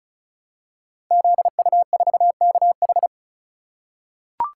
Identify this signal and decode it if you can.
Morse code, heard as a keyed tone, about 36 wpm, 700 Hz. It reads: ZU4KH
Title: Code Groups - 5 Letter:Numbers - 35wpm